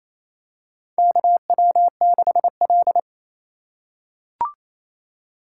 Morse code, heard as a keyed tone, about 28 wpm, 700 Hz. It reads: KW6L
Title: USA Call Signs - 4 Characters - 2x Character Spacing - 28wpm